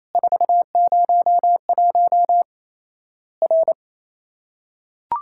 Morse code, heard as a keyed tone, about 27 wpm, 700 Hz. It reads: 401 R
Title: Worked All Europe DX Contest - QTC Exchange - 28wpm